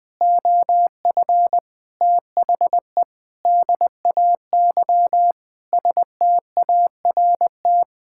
Morse code, heard as a keyed tone, about 21 wpm, 700 Hz. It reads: OF THE DAY START
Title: Sentences with 7-Word Phrases - 20wpm